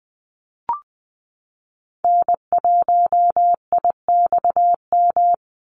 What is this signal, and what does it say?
Morse code, heard as a keyed tone, about 19 wpm, 700 Hz. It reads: N1IXM
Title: USA Call Signs 20wpm